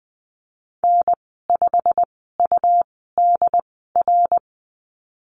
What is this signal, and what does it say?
Morse code, heard as a keyed tone, about 19 wpm, spaced as for 14 wpm, 700 Hz. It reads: N5UDR
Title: 179 Lesson 29g - Slash - Call Signs 20wpm